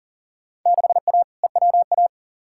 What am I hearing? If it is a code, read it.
Morse code, heard as a keyed tone, about 40 wpm, 700 Hz. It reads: 6A EWA
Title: Field Day Exchanges Encore - 40wpm